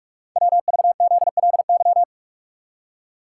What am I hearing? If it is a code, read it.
Morse code, heard as a keyed tone, about 44 wpm, 690 Hz. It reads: WVZLY
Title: Code Groups - 5 Letters - 45wpm